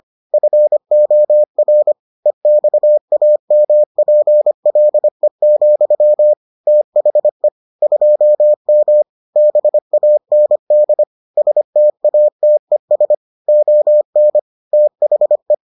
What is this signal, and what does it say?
Morse code, heard as a keyed tone, about 25 wpm, 600 Hz. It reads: FOR EXAMPLE, THE 2M BAND STATES ON THE